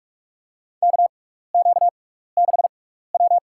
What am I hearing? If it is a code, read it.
Morse code, heard as a keyed tone, about 43 wpm, spaced as for 15 wpm, 700 Hz. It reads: KQ6W
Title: USA Call Signs - 4 Characters - 6x Character Spacing - 45wpm